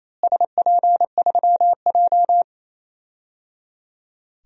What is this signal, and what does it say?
Morse code, heard as a keyed tone, about 28 wpm, 700 Hz. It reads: SP3J
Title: DXCC-100 Call Signs - 4 Characters - Repeated 3x - 28wpm